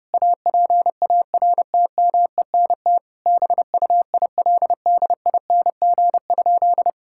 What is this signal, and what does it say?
Morse code, heard as a keyed tone, about 32 wpm, 700 Hz. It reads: APARTMENT BUILDING?
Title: Sentences from Top 1000 Words - Review Part 6 - 30wpm